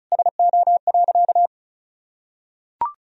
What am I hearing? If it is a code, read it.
Morse code, heard as a keyed tone, about 36 wpm, 700 Hz. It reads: SO.
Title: Sentences from Top 1000 Words - Review Part 1 - 35wpm